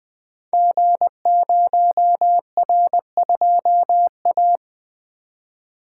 Morse code, heard as a keyed tone, about 20 wpm, 700 Hz. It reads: G0R2A
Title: Code Groups - 5 Letter:Numbers - 20wpm